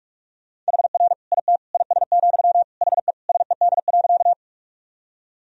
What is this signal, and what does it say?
Morse code, heard as a keyed tone, about 46 wpm, 700 Hz. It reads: HR IT IS, HE SED.
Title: Sentences from Top 200 Words and Abbreviations 45wpm